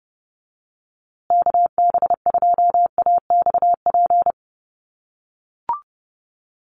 Morse code, heard as a keyed tone, about 30 wpm, 700 Hz. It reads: KB2AXP
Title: USA Call Signs Repeated 2x 30wpm